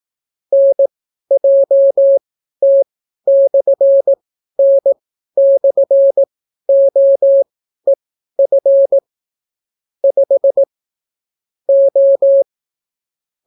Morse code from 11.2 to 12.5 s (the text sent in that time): O